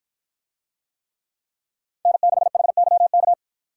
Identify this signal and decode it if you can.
Morse code, heard as a keyed tone, about 51 wpm, 700 Hz. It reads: N6HYK